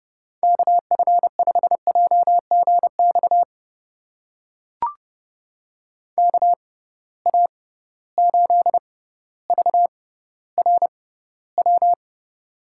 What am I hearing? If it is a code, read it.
Morse code, heard as a keyed tone, about 30 wpm, spaced as for 15 wpm, 700 Hz. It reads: KF5JGX KA8VRW
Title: USA Call Signs - 6 Characters - 6x Character Spacing - 30wpm